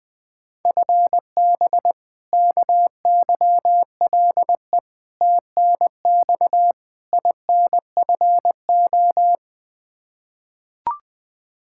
Morse code, heard as a keyed tone, about 20 wpm, 700 Hz. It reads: FB KYLE TNX INFO
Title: Practice QSO - Exchange 3 - More Info - 20wpm